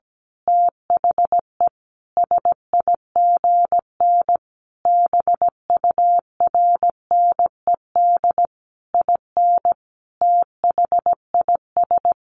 Morse code, heard as a keyed tone, about 17 wpm, 700 Hz. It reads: THE SIGN BURNED IN THIS
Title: A Sound Of Thunder 17wpm